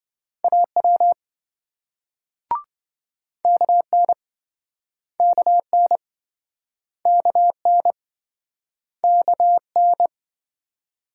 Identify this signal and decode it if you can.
Morse code, heard as a keyed tone, about 28 wpm, 700 Hz. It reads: AW KN KN KN KN